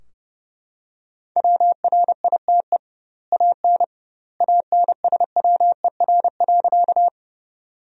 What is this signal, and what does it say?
Morse code, heard as a keyed tone, about 30 wpm, 700 Hz. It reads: WRITE AN ANSWER.
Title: 2x Word-Spacing - Sentences from Top 300 Words - 30wpm